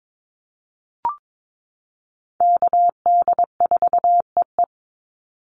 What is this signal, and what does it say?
Morse code, heard as a keyed tone, about 22 wpm, 700 Hz. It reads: KD4EE
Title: Call Signs - 22wpm